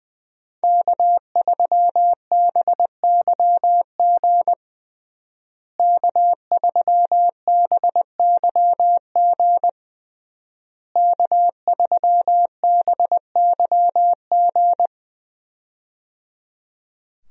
Morse code, heard as a keyed tone, about 21 wpm, 700 Hz. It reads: K3BYG K3BYG K3BYG
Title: USA Call Signs - 5 Characters - Repeated 3x - 20wpm